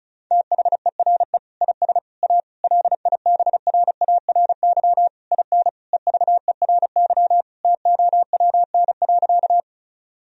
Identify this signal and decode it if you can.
Morse code, heard as a keyed tone, about 36 wpm, 700 Hz. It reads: THERE IS A LIBRARY IN EVERY TOWN.
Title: Sentences from Top 1000 Words - Review Part 4 - 35wpm